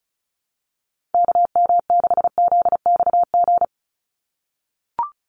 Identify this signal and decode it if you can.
Morse code, heard as a keyed tone, about 34 wpm, 700 Hz. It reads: KM6ZXG
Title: USA Call Signs 35wpm